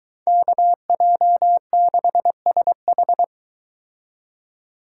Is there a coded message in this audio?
Yes: Morse code from 0.3 to 3.2 s, heard as a keyed tone, about 23 wpm, 700 Hz. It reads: KJ6SH